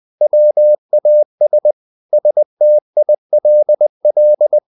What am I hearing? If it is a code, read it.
Morse code, heard as a keyed tone, about 20 wpm, 600 Hz. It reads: WAS STILL
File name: the old man and the sea part 2 20wpm